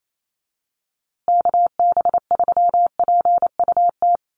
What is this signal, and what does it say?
Morse code, heard as a keyed tone, about 28 wpm, 700 Hz. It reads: KB3PUT
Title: USA Call Signs Repeated 3x 28wpm